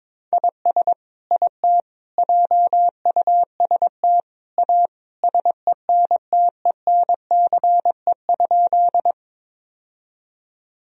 Morse code, heard as a keyed tone, about 22 wpm, 700 Hz. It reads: IS IT JUST A SENTENCE?
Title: Sentences from Top 1000 Words - Review Part 1 - 22wpm